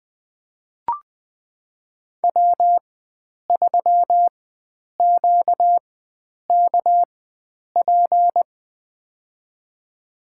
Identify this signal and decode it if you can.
Morse code, heard as a keyed tone, about 20 wpm, spaced as for 9 wpm, 700 Hz. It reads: W3QKP
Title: USA Call Signs - 4x Character Spacing - 20wpm